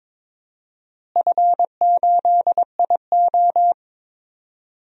Morse code, heard as a keyed tone, about 22 wpm, 700 Hz. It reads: F8IO
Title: DXCC-100 Call Signs - 4 Characters - Repeated 3x - 22wpm